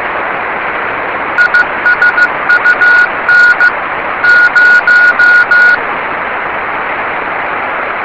Morse code, heard as a keyed tone, about 15 wpm, 1450 Hz. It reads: ISUN 0
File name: yubileiry cw by dk3wn